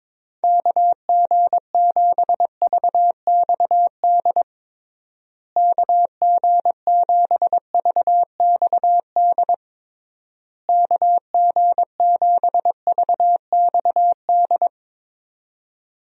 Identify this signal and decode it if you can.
Morse code, heard as a keyed tone, about 22 wpm, 700 Hz. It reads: KG7VXD KG7VXD KG7VXD
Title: USA Call Signs - 6 Characters - Repeated 3x - 22wpm